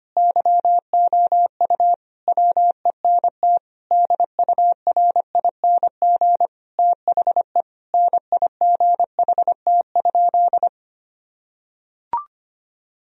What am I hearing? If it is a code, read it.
Morse code, heard as a keyed tone, about 25 wpm, 700 Hz. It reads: YOU WENT DURING THE NIGHT?
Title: Sentences from Top 1000 Words - Review Part 2 - 25wpm